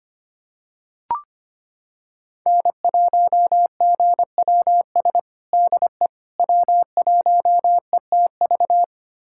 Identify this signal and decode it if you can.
Morse code, heard as a keyed tone, about 25 wpm, 700 Hz. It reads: N1GWS DE W1ETV
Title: USA Callsign DE Callsign - 5 Characters - 25wpm